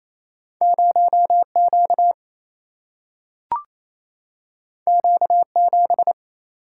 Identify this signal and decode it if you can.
Morse code, heard as a keyed tone, about 28 wpm, 700 Hz. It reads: 0 Q Q 7
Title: Single-Letter Single-Number 28wpm